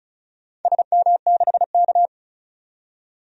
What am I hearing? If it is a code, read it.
Morse code, heard as a keyed tone, about 36 wpm, 700 Hz. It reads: SM6K
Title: DXCC-100 Call Signs - 4 Characters - 35wpm